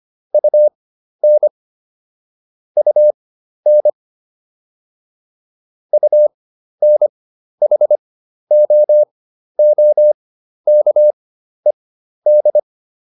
Morse code, heard as a keyed tone, about 25 wpm, spaced as for 12 wpm, 600 Hz. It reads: UN UN UNHOOKED